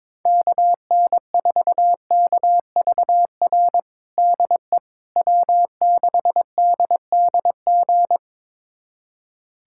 Morse code, heard as a keyed tone, about 22 wpm, 700 Hz. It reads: KN4KVR DE W6DDG
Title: USA Callsign DE Callsign 22wpm